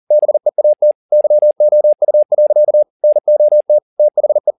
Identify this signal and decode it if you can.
Morse code, heard as a keyed tone, about 40 wpm, 600 Hz. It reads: BEAT YOU. NOT THE
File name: the old man and the sea 2 40wpm